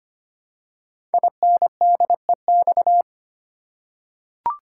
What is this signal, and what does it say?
Morse code, heard as a keyed tone, about 25 wpm, 700 Hz. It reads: INDEX
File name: 5 Letter Words - First 3 Letters Given - 25wpm